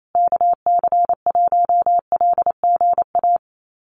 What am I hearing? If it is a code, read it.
Morse code, heard as a keyed tone, about 28 wpm, 700 Hz. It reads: KC1LGA
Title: USA Call Signs Repeated 3x 28wpm